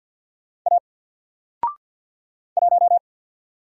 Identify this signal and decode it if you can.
Morse code, heard as a keyed tone, about 50 wpm, 700 Hz. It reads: A 1
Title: Single Letter-Number Alt 50wpm